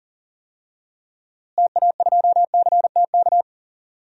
Morse code, heard as a keyed tone, about 40 wpm, 700 Hz. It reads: TA2CTK
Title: DXCC-100 Call Signs - 6 Characters - Repeated 3x - 40wpm